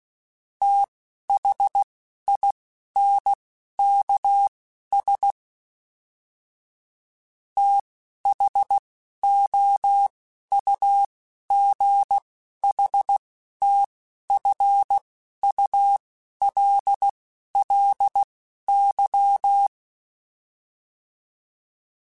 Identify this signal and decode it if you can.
Morse code, heard as a keyed tone, about 16 wpm, spaced as for 11 wpm, 800 Hz. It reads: THINKS THOUGHTFULLY
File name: TH Runs / TH Runs mp3s / TH Runs @10